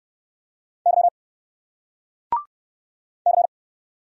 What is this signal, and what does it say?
Morse code, heard as a keyed tone, about 60 wpm, 700 Hz. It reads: Y C